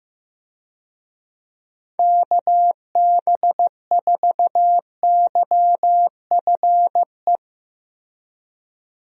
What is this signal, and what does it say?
Morse code, heard as a keyed tone, about 16 wpm, 700 Hz. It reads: KB4YFE
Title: USA Call Signs - 6 Characters - Repeated 2x - 15wpm